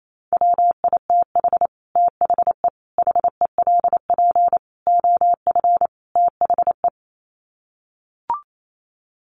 Morse code, heard as a keyed tone, about 28 wpm, 700 Hz. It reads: WITH THE HELP OF THE
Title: Sets of 5 Words Encore 28wpm